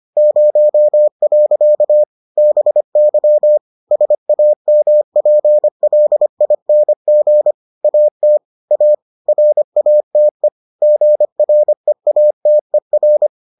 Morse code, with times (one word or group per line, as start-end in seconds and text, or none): 0.2-2.0 0.
2.4-3.6 BY
3.9-7.5 SAMPLING
7.8-8.4 AT
8.7-8.9 A
9.3-10.5 RATE
10.8-13.3 GREATER